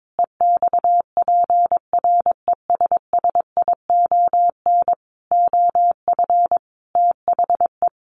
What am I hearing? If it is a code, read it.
Morse code, heard as a keyed tone, about 22 wpm, 700 Hz. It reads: EXPRESSION OF THE
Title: Sentences from Top 1400 Words 22wpm